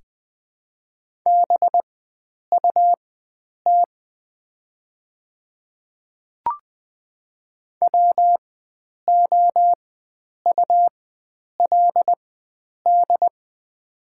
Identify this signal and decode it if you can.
Morse code, heard as a keyed tone, about 20 wpm, spaced as for 9 wpm, 700 Hz. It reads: BUT WOULD
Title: 4x Character Spacing - Top 100 Words - 20wpm